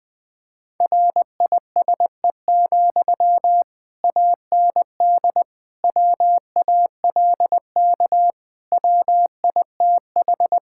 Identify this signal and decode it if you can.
Morse code, heard as a keyed tone, about 21 wpm, 700 Hz. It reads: RISE, AND WALK WITH
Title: Sentences from Top 1000 Words - Review Part 6 - 20wpm